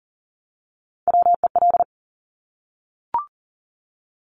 Morse code, heard as a keyed tone, about 42 wpm, 700 Hz. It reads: WEL
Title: Top Three Letter Combinations in Words 40wpm